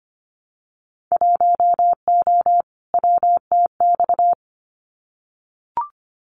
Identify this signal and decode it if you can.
Morse code, heard as a keyed tone, about 25 wpm, 700 Hz. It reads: 1O WTX
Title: Winter Field Day Exchanges Encore 25wpm